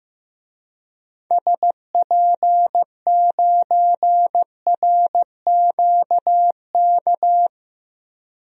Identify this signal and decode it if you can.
Morse code, heard as a keyed tone, about 15 wpm, 700 Hz. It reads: SP9RQK
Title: DXCC-100 Call Signs - 6 Characters - 15wpm